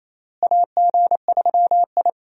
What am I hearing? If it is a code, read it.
Morse code, heard as a keyed tone, about 27 wpm, 700 Hz. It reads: AG3I
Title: USA Call Signs - 4 Characters - Repeated 2x - 28wpm